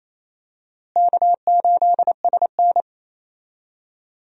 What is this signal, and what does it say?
Morse code, heard as a keyed tone, about 28 wpm, 700 Hz. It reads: K8SN
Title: USA Call Signs - 4 Characters - Repeated 2x - 28wpm